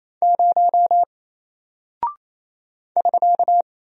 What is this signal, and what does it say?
Morse code, heard as a keyed tone, about 28 wpm, 700 Hz. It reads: 0 <SK>
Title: Single Character (Letter:Number:Punctuation:Prosign) Alt 28wpm